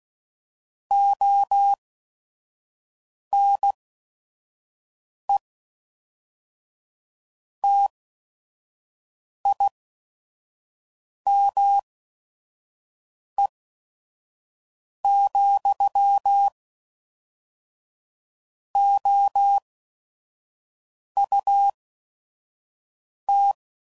Morse code, heard as a keyed tone, about 16 wpm, spaced as for 5 wpm, 800 Hz. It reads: ONE TIME, OUT
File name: CW Code Groups @5WPM / Technical Subjects / AND Gate